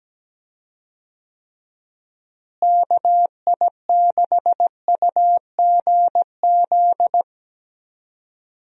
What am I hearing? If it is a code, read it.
Morse code, heard as a keyed tone, about 17 wpm, 700 Hz. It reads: KI6UGZ
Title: USA Call Signs - 6 Characters - Repeated 2x - 17wpm